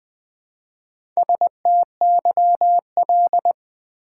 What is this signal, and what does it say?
Morse code, heard as a keyed tone, about 20 wpm, 700 Hz. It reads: STYL
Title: Build A Word - 1 Letter At A Time - 5-Letter Words - 20wpm